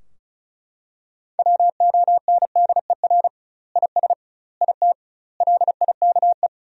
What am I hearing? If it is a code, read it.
Morse code, heard as a keyed tone, about 36 wpm, 700 Hz. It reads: WONDER IS IT LIKE
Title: 2x Word-Spacing - Sentences from Top 500 Words - 35wpm